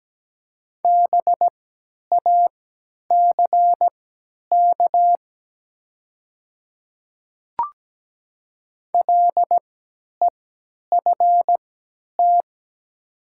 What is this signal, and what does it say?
Morse code, heard as a keyed tone, about 17 wpm, spaced as for 10 wpm, 700 Hz. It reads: BACK LEFT
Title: 3x Character Spacing - Top 300 Words - 17wpm